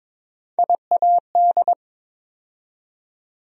Morse code, heard as a keyed tone, about 22 wpm, 700 Hz. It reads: IAD
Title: USA Call Sign Postfixes - 3 Letters 22wpm